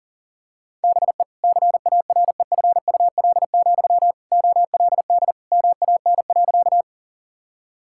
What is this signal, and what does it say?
Morse code, heard as a keyed tone, about 37 wpm, 700 Hz. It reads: BE CAREFUL, OLD MAN.
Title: Sentences from Top 1600 Words 40wpm